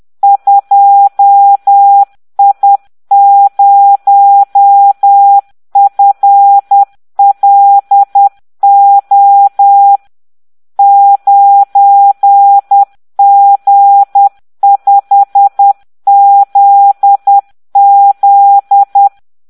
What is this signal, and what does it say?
Morse code, heard as a keyed tone, about 10 wpm, 800 Hz. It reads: 2I0FLO 9G5ZZ